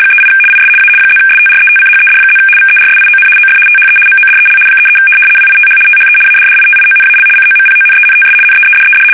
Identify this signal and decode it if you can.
RTTY, about 75 baud. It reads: 38 IS A +STRETCHED+ VERSION OF THE MODEL 33 WITH THE ADDITIONAL COMPONENTS NEEDED
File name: fsk850-75B / teletype-corp